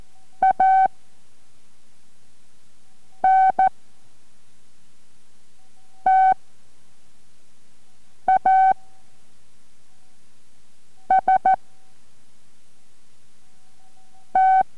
Morse code, heard as a keyed tone, about 14 wpm, 760 Hz. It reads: A N T A S T